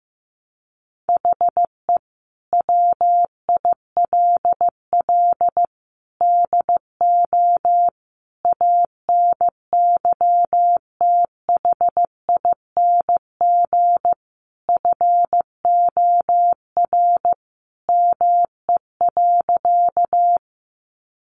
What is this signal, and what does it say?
Morse code, heard as a keyed tone, about 15 wpm, 700 Hz. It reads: HE WILL DO ANYTHING FOR ME.
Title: Sentences from Top 1000 Words 15wpm